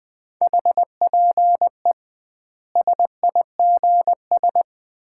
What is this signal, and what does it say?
Morse code, heard as a keyed tone, about 21 wpm, 700 Hz. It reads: HPE SIGS
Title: Common QSO Phrases - 2x Word-Spacing - 20wpm